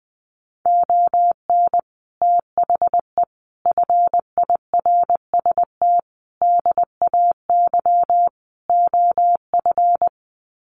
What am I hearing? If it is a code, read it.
Morse code, heard as a keyed tone, about 19 wpm, 700 Hz. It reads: ON THE FIRST DAY OF
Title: Sets of 5 Words Encore 20wpm